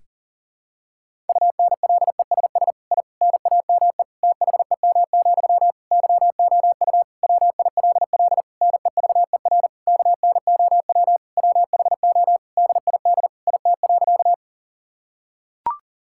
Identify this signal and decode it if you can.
Morse code, heard as a keyed tone, about 40 wpm, 700 Hz. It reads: UNLESS I NAME THEM, YOU WILL NEVER KNOW WHO DID IT.
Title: Sentences from Top 1000 Words - Review Part 6 - 40wpm